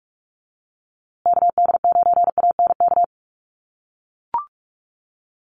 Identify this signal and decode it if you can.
Morse code, heard as a keyed tone, about 44 wpm, 700 Hz. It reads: KD9ANK
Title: USA Call Signs Repeated 3x 45wpm